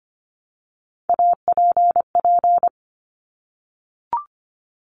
Morse code, heard as a keyed tone, about 25 wpm, 700 Hz. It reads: APP
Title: Top Three Letter Combinations in Words 25wpm